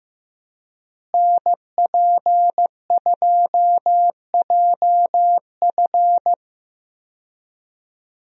Morse code, heard as a keyed tone, about 15 wpm, 700 Hz. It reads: NP2JF
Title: USA Call Signs - 5 Characters - Repeated 2x - 15wpm